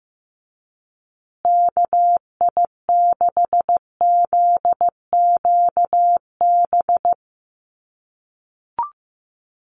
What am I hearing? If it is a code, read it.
Morse code, heard as a keyed tone, about 15 wpm, 700 Hz. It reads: KI6ZQB
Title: USA Call Signs Repeated 2x 15wpm